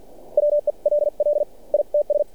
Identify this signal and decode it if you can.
Morse code, heard as a keyed tone, about 46 wpm, spaced as for 39 wpm, 590 Hz. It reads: WELL ITS
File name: archived / CRSnet